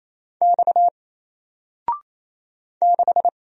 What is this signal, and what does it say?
Morse code, heard as a keyed tone, about 27 wpm, 700 Hz. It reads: X 6
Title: Single Letter-Number Alt 28wpm